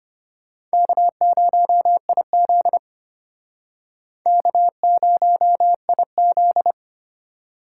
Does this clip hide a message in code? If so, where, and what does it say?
Yes: Morse code from 0.7 to 6.7 s, heard as a keyed tone, about 32 wpm, 700 Hz. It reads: K0IZ K0IZ